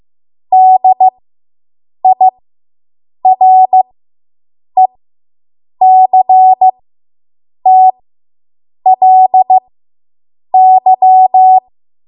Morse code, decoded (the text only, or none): DIRECTLY